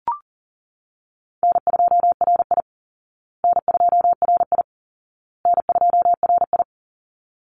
Morse code, heard as a keyed tone, about 40 wpm, 700 Hz. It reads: N2RI N2RI N2RI
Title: Easy Call Signs Repeated 3x - 40wpm